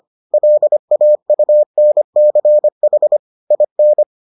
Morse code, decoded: LAUNCH IN

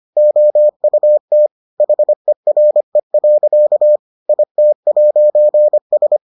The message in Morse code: OUT HERE. IT'S

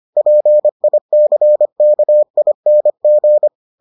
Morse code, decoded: PICKING

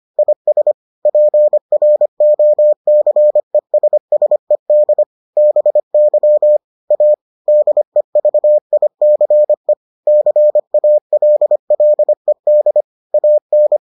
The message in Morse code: IS PROCESSED BY A DEVICE CALLED AN

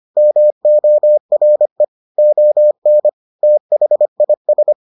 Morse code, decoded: MORE ON THIS

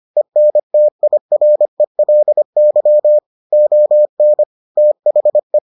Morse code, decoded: ENTIRELY ON THE

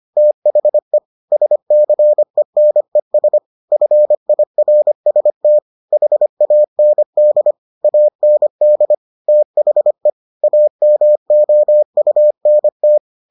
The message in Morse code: THE SCENES FIRST HAND AND THE AMOUNT

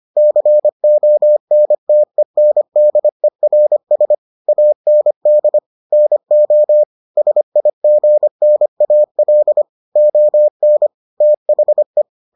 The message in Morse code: CONTENDERS AND NO SIGNAL ON THE

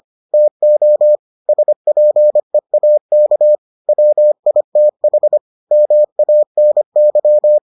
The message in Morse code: TO SPEAK WITH MANY

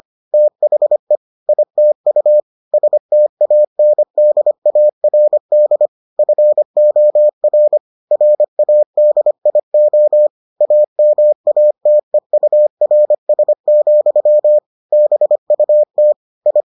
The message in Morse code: THE ITU STANDARD FOR RADIO AMATEURS, BUT I

